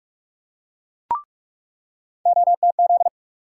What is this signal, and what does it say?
Morse code, heard as a keyed tone, about 46 wpm, 700 Hz. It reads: OTZ
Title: USA Call Sign Postfixes - 3 Letters 45wpm